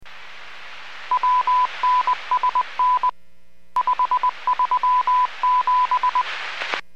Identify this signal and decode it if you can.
Morse code, heard as a keyed tone, about 20 wpm, 1000 Hz. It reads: WNSN 537